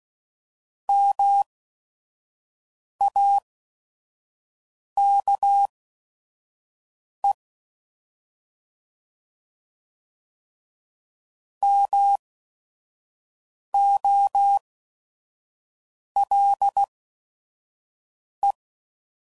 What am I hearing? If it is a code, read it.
Morse code, heard as a keyed tone, about 16 wpm, spaced as for 5 wpm, 800 Hz. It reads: MAKE MOLE